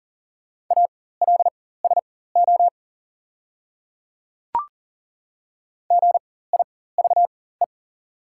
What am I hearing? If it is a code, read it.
Morse code, heard as a keyed tone, about 40 wpm, spaced as for 19 wpm, 690 Hz. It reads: ALSO GIVE
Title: 4x Character Spacing - Top 300 Words - 40wpm